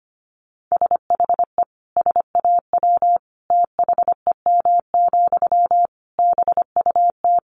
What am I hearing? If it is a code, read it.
Morse code, heard as a keyed tone, about 25 wpm, 700 Hz. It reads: SHE SAW THEM, BUT
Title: Sentences from Top 1000 Words 25wpm